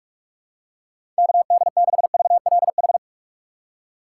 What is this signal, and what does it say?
Morse code, heard as a keyed tone, about 42 wpm, 700 Hz. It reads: KD6VLH
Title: USA Call Signs - 6 Characters - 3x Character Spacing - 45wpm